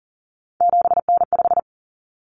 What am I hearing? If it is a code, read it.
Morse code, heard as a keyed tone, about 40 wpm, 700 Hz. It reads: 7N5
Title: Common Cut Numbers - 3-Digit 40wpm